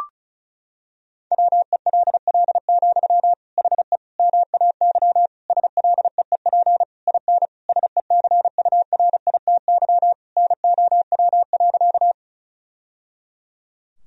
Morse code, heard as a keyed tone, about 36 wpm, 700 Hz. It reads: WELL, HE MAY SLEEP IN SECURITY NOW.
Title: Sentences from Top 1000 Words - Review Part 4 - 35wpm